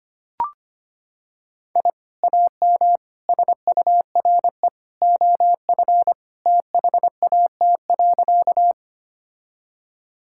Morse code, heard as a keyed tone, about 25 wpm, 700 Hz. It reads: I AM SURE OF THAT.
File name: practice / 25wpm / Sentences with 2-Word Phrases - 25wpm